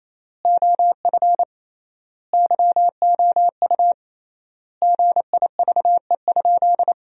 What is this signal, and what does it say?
Morse code, heard as a keyed tone, about 28 wpm, 700 Hz. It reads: OF YOU GIVE?
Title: 3x Word-Spacing - Sentences from Top 200 Words - 28wpm